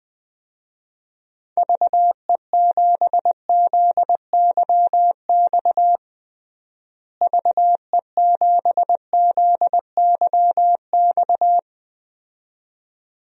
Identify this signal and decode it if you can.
Morse code, heard as a keyed tone, about 20 wpm, 700 Hz. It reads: VE7ZYX VE7ZYX
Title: DXCC-100 Call Signs - 6 Characters - Repeated 2x - 20wpm